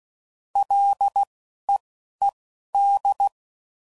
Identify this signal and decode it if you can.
Morse code, heard as a keyed tone, about 16 wpm, spaced as for 11 wpm, 800 Hz. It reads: LEED
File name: Prefixes / FLs & THs / L Prefixes